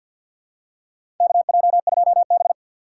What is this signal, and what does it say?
Morse code, heard as a keyed tone, about 50 wpm, 700 Hz. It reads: KJ2B
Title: USA Call Signs - 4 Characters - 6x Character Spacing - 50wpm